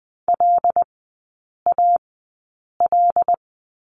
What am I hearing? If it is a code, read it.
Morse code, heard as a keyed tone, about 20 wpm, spaced as for 8 wpm, 690 Hz. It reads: LAL
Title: 081 Lesson 14e - Number 2 - Three Characters 20wpm